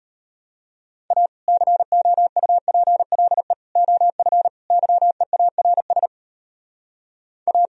Morse code, heard as a keyed tone, about 32 wpm, 700 Hz. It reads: A COUPLE OF YEARS A